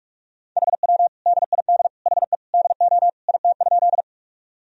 Morse code, heard as a keyed tone, about 47 wpm, 700 Hz. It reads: HW DID HE DO IT?
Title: Sentences from Top 100 Words and Abbreviations 45wpm